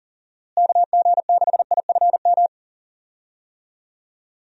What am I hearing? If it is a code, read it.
Morse code, heard as a keyed tone, about 38 wpm, 700 Hz. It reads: KG6IFM
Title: USA Call Signs - 6 Characters - Repeated 2x - 40wpm